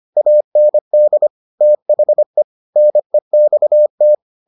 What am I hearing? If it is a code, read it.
Morse code, heard as a keyed tone, about 25 wpm, 600 Hz. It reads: AND THE NEXT